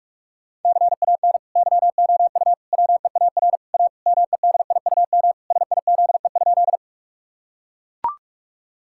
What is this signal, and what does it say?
Morse code, heard as a keyed tone, about 42 wpm, 700 Hz. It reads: CAN YOU WEAR A MEDIUM SIZE?
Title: Sentences from Top 1000 Words - Review Part 6 - 45wpm